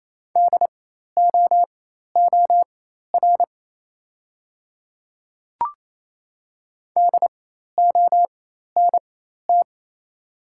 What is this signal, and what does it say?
Morse code, heard as a keyed tone, about 28 wpm, spaced as for 13 wpm, 700 Hz. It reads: DOOR DONT
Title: 4x Character Spacing - Top 300 Words - 28wpm